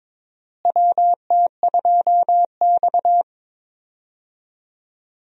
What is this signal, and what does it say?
Morse code, heard as a keyed tone, about 22 wpm, 700 Hz. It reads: WT2X
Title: USA Call Signs - 4 Characters - Repeated 2x - 22wpm